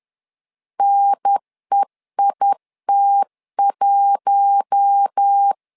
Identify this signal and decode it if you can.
Morse code, heard as a keyed tone, about 10 wpm, 800 Hz. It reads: NEIT1
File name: LEZIONI / 4 Q1 10wpm 2448